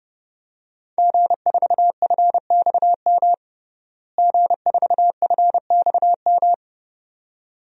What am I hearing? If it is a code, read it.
Morse code, heard as a keyed tone, about 30 wpm, 700 Hz. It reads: G4FXM G4FXM
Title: DXCC-100 Call Signs - 5 Characters - Repeated 2x - 30wpm